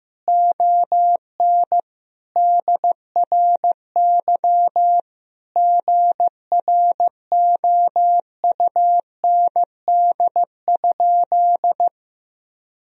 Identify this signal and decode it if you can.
Morse code, heard as a keyed tone, about 15 wpm, 700 Hz. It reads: ON DRY GROUND?
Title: Sentences from Top 1000 Words - Review Part 3 - 15wpm